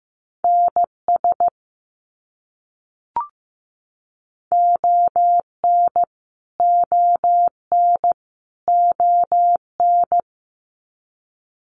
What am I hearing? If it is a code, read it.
Morse code, heard as a keyed tone, about 15 wpm, 700 Hz. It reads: NS ON ON ON
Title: Canadian Provinces and Territories - Repeated 3x - 15wpm